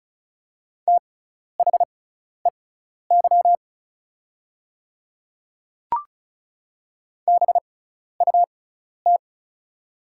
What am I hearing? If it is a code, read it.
Morse code, heard as a keyed tone, about 36 wpm, spaced as for 12 wpm, 700 Hz. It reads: THEY BUT
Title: 6x Character Spacing - Top 100 Words - 35wpm